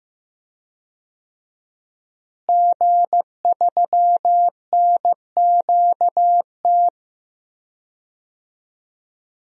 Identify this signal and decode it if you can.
Morse code, heard as a keyed tone, about 15 wpm, 700 Hz. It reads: G3NQT
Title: DXCC-100 Call Signs - 5 Characters - Repeated 2x - 15wpm